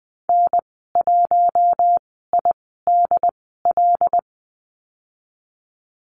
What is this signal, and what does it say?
Morse code, heard as a keyed tone, about 20 wpm, spaced as for 14 wpm, 700 Hz. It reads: N1IDL
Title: 076 Lesson 13g - Letter L - Call Signs 20wpm